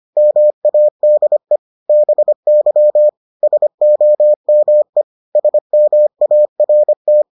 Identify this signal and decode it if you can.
Morse code, heard as a keyed tone, about 25 wpm, 600 Hz. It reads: MADE BY SOME SMART